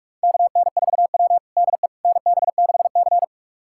Morse code, heard as a keyed tone, about 46 wpm, 700 Hz. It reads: KN4W DE NB6C
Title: USA Callsign DE Callsign - 4 Characters - 45wpm